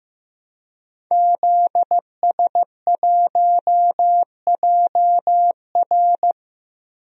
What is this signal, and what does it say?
Morse code, heard as a keyed tone, about 15 wpm, 700 Hz. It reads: ZS1JR
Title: DXCC-100 Call Signs - 5 Characters - Repeated 3x - 15wpm